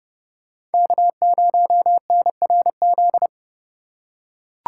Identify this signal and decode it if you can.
Morse code, heard as a keyed tone, about 28 wpm, 700 Hz. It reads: K0NRZ